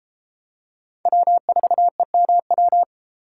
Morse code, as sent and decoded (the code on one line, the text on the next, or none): .-- ....- . -- .--
W4EMW